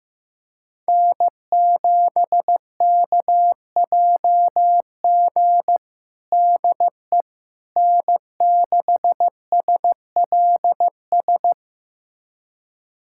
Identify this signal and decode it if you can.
Morse code, heard as a keyed tone, about 15 wpm, 700 Hz. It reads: N7KJG DE N6SLS
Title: USA Callsign DE Callsign - 5 Characters - 15wpm